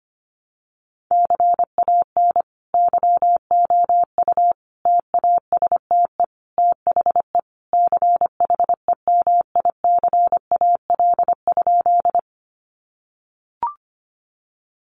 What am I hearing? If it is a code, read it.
Morse code, heard as a keyed tone, about 25 wpm, 700 Hz. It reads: CAN YOU TASTE THE CHEMICAL?
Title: Sentences from Top 1200 Words 25wpm